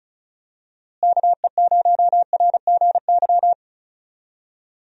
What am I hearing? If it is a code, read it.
Morse code, heard as a keyed tone, about 36 wpm, 700 Hz. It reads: KE0RGY
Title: USA Call Signs - 6 Characters - Repeated 2x - 35wpm